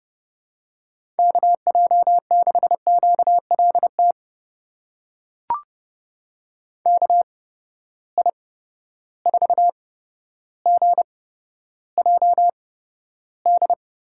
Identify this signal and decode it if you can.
Morse code, heard as a keyed tone, about 30 wpm, spaced as for 13 wpm, 700 Hz. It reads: KJ6QLT KI4GJD
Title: USA Call Signs - 6 Characters - 8x Character Spacing - 30wpm